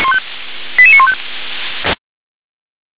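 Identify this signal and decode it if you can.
Five-tone selective call (ZVEI), heard as quick runs of five tones, tones of about 70 ms. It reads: 88915
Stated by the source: ZVEI five-tone selective call 88786 received on SDR.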